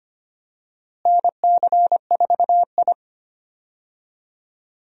Morse code, heard as a keyed tone, about 25 wpm, 700 Hz. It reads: NC4I
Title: USA Call Signs - 4 Characters - Repeated 2x - 25wpm